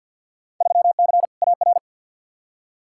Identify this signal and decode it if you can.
Morse code, heard as a keyed tone, about 52 wpm, 700 Hz. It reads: 3C AR
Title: Field Day Exchanges Encore - 50wpm